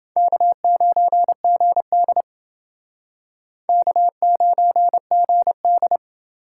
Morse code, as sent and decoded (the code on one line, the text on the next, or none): -.- ----. --. -.. / -.- ----. --. -..
K9GD K9GD